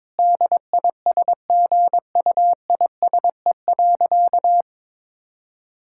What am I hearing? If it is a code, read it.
Morse code, heard as a keyed tone, about 22 wpm, 700 Hz. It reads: DISGUISE.
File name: Difficult Sentences with Hints - 22wpm